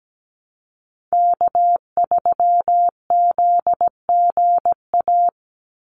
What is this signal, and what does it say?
Morse code, heard as a keyed tone, about 17 wpm, 700 Hz. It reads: K3ZGA
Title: Call Signs Repeated 3x - 17wpm